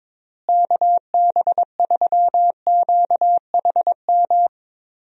Morse code, heard as a keyed tone, about 22 wpm, 700 Hz. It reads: KB3QHM
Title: USA Call Signs - 6 Characters - Repeated 3x - 22wpm